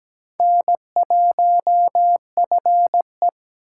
Morse code, heard as a keyed tone, about 17 wpm, 700 Hz. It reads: N1FE
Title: K1USN SST QSO Practice - 17wpm